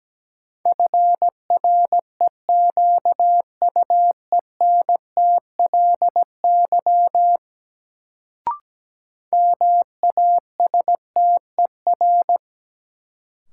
Morse code, heard as a keyed tone, about 17 wpm, 700 Hz. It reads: FREQUENTLY MASTER
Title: Sentences from Top 1300 Words - New Word Drill - 17wpm